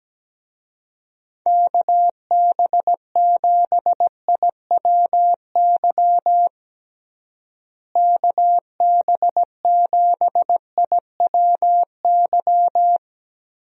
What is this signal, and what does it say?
Morse code, heard as a keyed tone, about 17 wpm, 700 Hz. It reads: KB7IWY KB7IWY
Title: USA Call Signs - 6 Characters - Repeated 3x - 17wpm